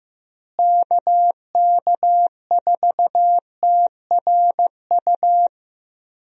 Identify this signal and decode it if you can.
Morse code, heard as a keyed tone, about 16 wpm, 700 Hz. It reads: KK4TRU
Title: USA Call Signs - 6 Characters - Repeated 3x - 15wpm